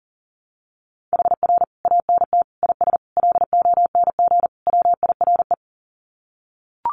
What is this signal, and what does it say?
Morse code, heard as a keyed tone, about 40 wpm, 700 Hz. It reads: HR ANT IS LONG WIRE
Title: Common QSO Phrases 40wpm